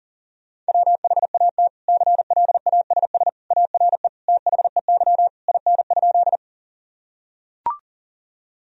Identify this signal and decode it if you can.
Morse code, heard as a keyed tone, about 40 wpm, 700 Hz. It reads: WHAT CLASS ARE THEY IN?
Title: Sentences from Top 1000 Words - Review Part 2 - 40wpm